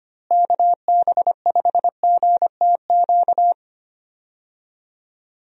Morse code, heard as a keyed tone, about 25 wpm, 700 Hz. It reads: KB5GTQ
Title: USA Call Signs - 6 Characters - Repeated 2x - 25wpm